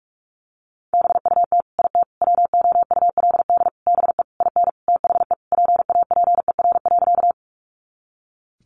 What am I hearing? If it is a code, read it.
Morse code, heard as a keyed tone, about 44 wpm, 700 Hz. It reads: BUT IT WOULD BE IN THE PAPER.
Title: Sentences from Top 300 Words 45wpm